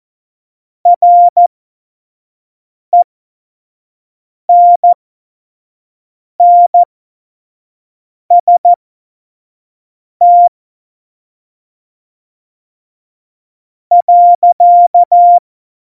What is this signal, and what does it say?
Morse code, heard as a keyed tone, about 14 wpm, spaced as for 4 wpm, 700 Hz. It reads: RENNST.
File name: Morsekurs / Witze / Witz 17 April 2018, 25 F70